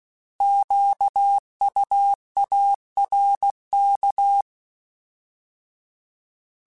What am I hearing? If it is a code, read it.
Morse code, heard as a keyed tone, about 16 wpm, 800 Hz. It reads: QUARK